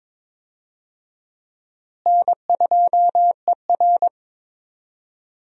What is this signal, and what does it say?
Morse code, heard as a keyed tone, about 22 wpm, 700 Hz. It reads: N2ER
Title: USA Call Signs - 4 Characters - Repeated 3x - 22wpm